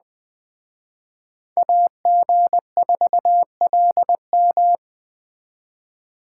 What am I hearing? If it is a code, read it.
Morse code, heard as a keyed tone, about 20 wpm, 700 Hz. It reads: AG4LM